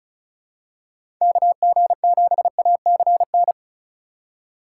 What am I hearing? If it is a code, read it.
Morse code, heard as a keyed tone, about 34 wpm, 700 Hz. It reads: KG7ACN
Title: USA Call Signs - 6 Characters - 2x Character Spacing - 35wpm